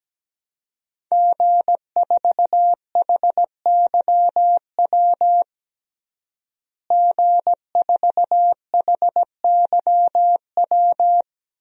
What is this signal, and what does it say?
Morse code, heard as a keyed tone, about 17 wpm, 700 Hz. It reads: G4HYW G4HYW
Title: DXCC-100 Call Signs - 5 Characters - Repeated 3x - 17wpm